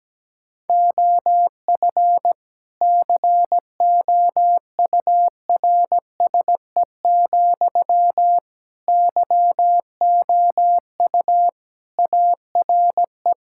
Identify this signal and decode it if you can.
Morse code, heard as a keyed tone, about 17 wpm, 700 Hz. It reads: OF COURSE, YOU ARE
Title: Sentences from Top 1000 Words - Review Part 6 - 17wpm